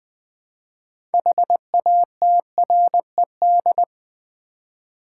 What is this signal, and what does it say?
Morse code, heard as a keyed tone, about 21 wpm, 700 Hz. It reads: HATRED
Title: 6 Letter Words - First 3 Letters Given - 20wpm